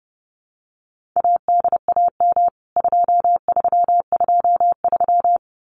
Morse code, heard as a keyed tone, about 30 wpm, 700 Hz. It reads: ADAM 2323
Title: CWT Exchange 30wpm